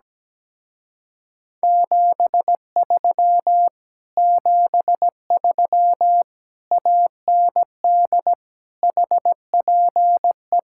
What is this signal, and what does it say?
Morse code, heard as a keyed tone, about 17 wpm, 700 Hz. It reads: 73 73 AND HPE
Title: Speed-Racing Common QSO Phrases - 17wpm